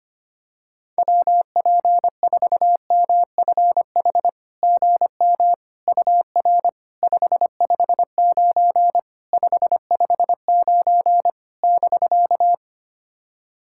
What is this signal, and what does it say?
Morse code, heard as a keyed tone, about 25 wpm, 700 Hz. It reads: WP4MFH GM UR 559 559 <BK>
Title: SOTA QSO Practice 25wpm